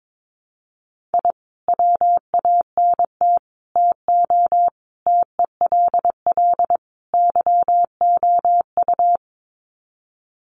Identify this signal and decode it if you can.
Morse code, heard as a keyed tone, about 22 wpm, 700 Hz. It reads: I WANT TO TELL YOU
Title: Sets of 5 Words Encore 22wpm